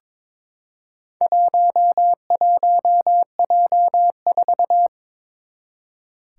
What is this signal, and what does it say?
Morse code, heard as a keyed tone, about 22 wpm, 700 Hz. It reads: 11J4
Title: Four Challenging Letter-Number Combinations 22wpm